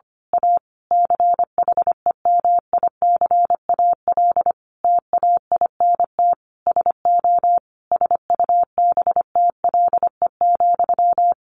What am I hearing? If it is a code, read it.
Morse code, heard as a keyed tone, about 25 wpm, 700 Hz. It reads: A CHEMICAL TAINT SO SUBTLE,
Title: A Sound Of Thunder 25wpm